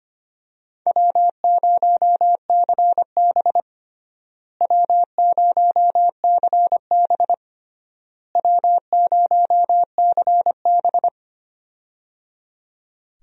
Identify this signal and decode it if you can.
Morse code, heard as a keyed tone, about 25 wpm, 700 Hz. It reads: W0CB W0CB W0CB
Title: USA Call Signs - 4 Characters - Repeated 3x - 25wpm